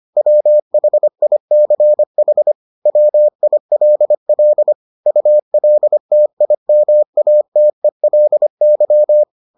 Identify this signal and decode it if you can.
Morse code, heard as a keyed tone, about 25 wpm, 600 Hz. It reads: WHICH WILL ULTIMATELY